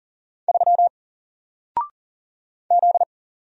Morse code, heard as a keyed tone, about 40 wpm, 700 Hz. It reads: 3 Z
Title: Single Letter-Number 40wpm